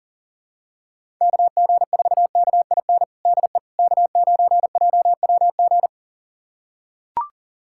Morse code, heard as a keyed tone, about 40 wpm, 700 Hz. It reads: KG4KIN DE K9JWG
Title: USA Callsign DE Callsign 40wpm